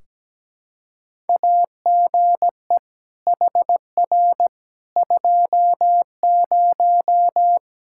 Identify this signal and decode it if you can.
Morse code, heard as a keyed tone, about 17 wpm, 700 Hz. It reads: AGE HR 20
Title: Practice QSO - Exchange 3 - More Info - 17wpm